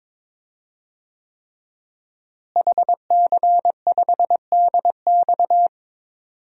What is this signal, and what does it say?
Morse code, heard as a keyed tone, about 22 wpm, 700 Hz. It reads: HC5DX
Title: DXCC-100 Call Signs - 5 Characters - Repeated 3x - 22wpm